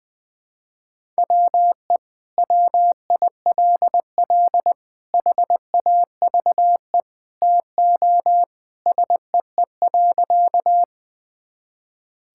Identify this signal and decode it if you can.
Morse code, heard as a keyed tone, about 21 wpm, 700 Hz. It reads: WE WILL HAVE TO SEE.
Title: Sentences with 3-Word Phrases - Spoken Beginning - 20wpm